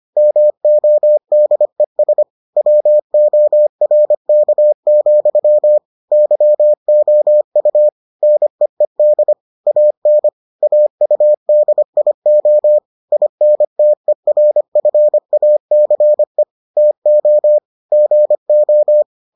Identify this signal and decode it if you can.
Morse code, heard as a keyed tone, about 25 wpm, 600 Hz. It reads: MODES WORK, YOU NEED AN AUDIO INTERFACE TO GO